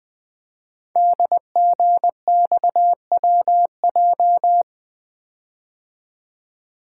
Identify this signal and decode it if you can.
Morse code, heard as a keyed tone, about 20 wpm, 700 Hz. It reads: DGXWJ
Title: Code Groups - 5 Letters - 20wpm